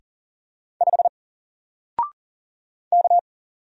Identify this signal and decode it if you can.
Morse code, heard as a keyed tone, about 40 wpm, 700 Hz. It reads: 5 K
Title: Single Character (Letter:Number:Punctuation:Prosign) Alt 40wpm